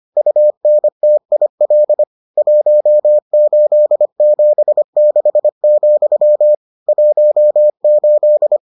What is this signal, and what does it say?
Morse code, heard as a keyed tone, about 25 wpm, 600 Hz. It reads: UNTIL 1876, 18